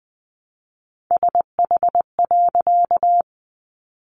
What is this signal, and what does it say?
Morse code, heard as a keyed tone, about 20 wpm, 700 Hz. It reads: SH.
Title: A Sound Of Thunder 20wpm